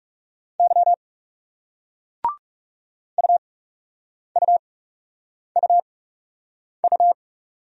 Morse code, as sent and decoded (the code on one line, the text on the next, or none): -.-- / ..- ..- ..- ..-
Y UUUU